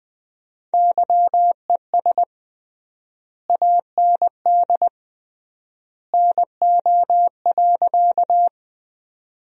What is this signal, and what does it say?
Morse code, heard as a keyed tone, about 21 wpm, 700 Hz. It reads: YES AND NO.
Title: 3x Word-Spacing - Sentences from Top 500 Words - 20wpm